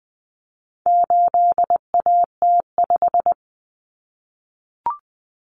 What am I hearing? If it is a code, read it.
Morse code, heard as a keyed tone, about 20 wpm, 700 Hz. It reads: 8AT5
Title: Common Cut Numbers - 4-Digit 20wpm